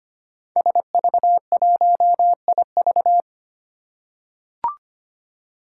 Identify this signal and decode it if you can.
Morse code, heard as a keyed tone, about 25 wpm, 700 Hz. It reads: SV1IV
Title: Five Challenging Letter-Number Combinations 25wpm